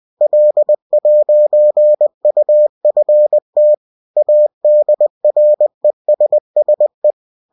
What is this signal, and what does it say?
Morse code, heard as a keyed tone, about 20 wpm, 600 Hz. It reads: L'UFT ADRESSE